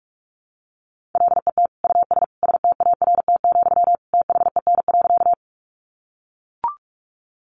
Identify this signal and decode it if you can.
Morse code, heard as a keyed tone, about 44 wpm, 700 Hz. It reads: LET US START, THEN.
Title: Sentences from Top 300 Words 45wpm